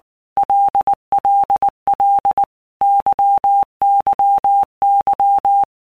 Morse code, heard as a keyed tone, about 19 wpm, 800 Hz. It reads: LLL YYY
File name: Basic Letter Groups By Blocks / Groups@18 / LFCYQ@18